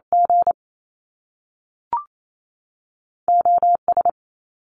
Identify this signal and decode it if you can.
Morse code, heard as a keyed tone, about 28 wpm, 700 Hz. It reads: G OS
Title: Top Two Letter Combinations in Words 28wpm